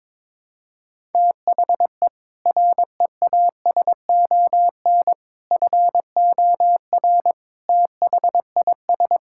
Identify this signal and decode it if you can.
Morse code, heard as a keyed tone, about 21 wpm, 700 Hz. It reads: THE REASON FOR THIS
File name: practice / 22wpm / Sets of 7 Words - 22wpm